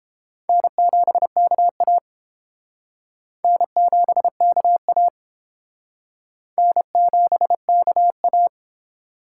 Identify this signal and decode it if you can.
Morse code, heard as a keyed tone, about 31 wpm, 700 Hz. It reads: N7KA N7KA N7KA